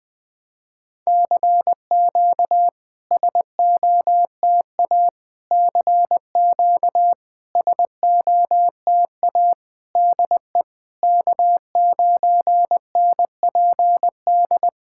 Morse code, heard as a keyed tone, about 20 wpm, 690 Hz. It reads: CQ SOTA CQ SOTA DE K9NPD
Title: SOTA QSO Practice 20wpm